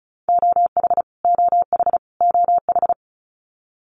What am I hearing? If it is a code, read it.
Morse code, heard as a keyed tone, about 34 wpm, 700 Hz. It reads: OH OH OH
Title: Top 500 Words Repeated 35wpm